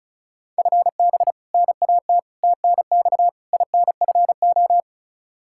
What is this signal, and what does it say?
Morse code, heard as a keyed tone, about 34 wpm, 700 Hz. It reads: FB NAT TNX INFO
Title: Practice QSO - Exchange 3 - More Info - 35wpm